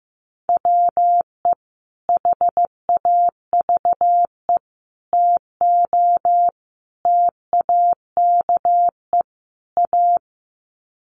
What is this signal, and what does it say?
Morse code, heard as a keyed tone, about 15 wpm, 700 Hz. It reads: WE HAVE TO TAKE A
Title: Sets of 5 Words Encore 15wpm